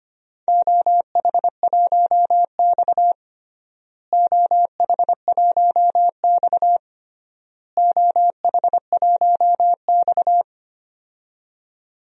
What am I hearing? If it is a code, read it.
Morse code, heard as a keyed tone, about 25 wpm, 700 Hz. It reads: OH1X OH1X OH1X
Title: DXCC-100 Call Signs - 4 Characters - Repeated 3x - 25wpm